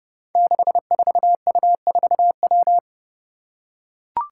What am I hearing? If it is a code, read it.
Morse code, heard as a keyed tone, about 32 wpm, 700 Hz. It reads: 64U4W
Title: Five Challenging Letter-Number Combinations 30wpm